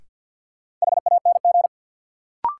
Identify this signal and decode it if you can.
Morse code, heard as a keyed tone, about 52 wpm, 700 Hz. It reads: HANG
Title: Sentences from Top 1600 Words - New Word Drill Alt - 50wpm